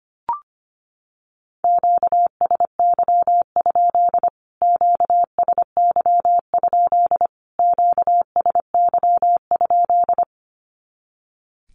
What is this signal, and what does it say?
Morse code, heard as a keyed tone, about 25 wpm, 700 Hz. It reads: QSY? QSY? QSY?
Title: Q-Codes Repeated 3x 25wpm